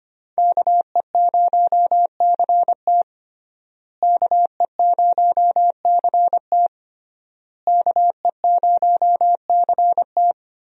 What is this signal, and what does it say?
Morse code, heard as a keyed tone, about 25 wpm, 700 Hz. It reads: KE0CT KE0CT KE0CT
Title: USA Call Signs - 5 Characters - Repeated 3x - 25wpm